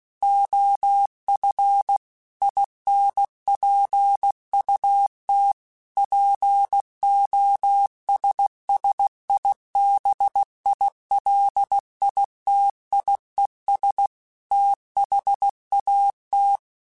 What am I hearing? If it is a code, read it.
Morse code, heard as a keyed tone, about 16 wpm, 800 Hz. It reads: OF INPUT POSSIBILITIES THAT
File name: Technical Subjects / Technical Subjects-mp3s / OR Gate @15